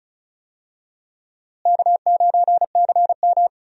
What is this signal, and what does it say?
Morse code, heard as a keyed tone, about 34 wpm, 700 Hz. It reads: K9CM
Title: USA Call Signs - 4 Characters - Repeated 2x - 35wpm